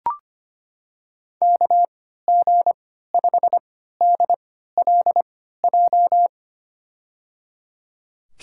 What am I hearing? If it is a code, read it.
Morse code, heard as a keyed tone, about 25 wpm, spaced as for 14 wpm, 700 Hz. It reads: KG5DLJ
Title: USA Call Signs - 6 Characters - 3x Character Spacing - 25wpm